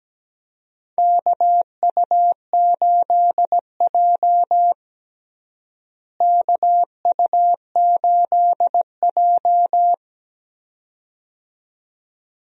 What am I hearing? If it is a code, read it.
Morse code, heard as a keyed tone, about 17 wpm, 700 Hz. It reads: KU8J KU8J
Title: USA Call Signs - 4 Characters - Repeated 2x - 17wpm